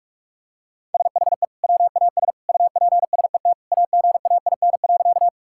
Morse code, heard as a keyed tone, about 46 wpm, 700 Hz. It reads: SHE WAS UPSET AGAIN.
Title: Sentences from Top 1000 Words - Review Part 5 - 45wpm